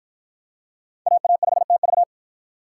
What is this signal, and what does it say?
Morse code, heard as a keyed tone, about 53 wpm, 700 Hz. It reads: AA5TV